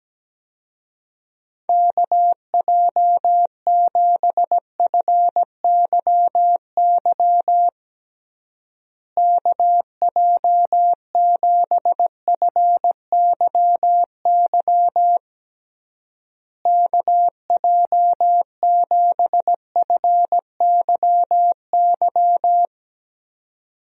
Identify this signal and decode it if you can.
Morse code, heard as a keyed tone, about 17 wpm, 700 Hz. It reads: KJ7FYY KJ7FYY KJ7FYY
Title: USA Call Signs - 6 Characters - Repeated 3x - 17wpm